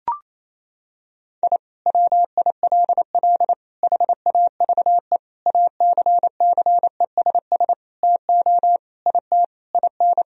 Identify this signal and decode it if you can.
Morse code, heard as a keyed tone, about 28 wpm, 700 Hz. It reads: I WILL HAVE ACCESS TO IT IN
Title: Sentences from Top 1000 Words - Review Part 4 - 28wpm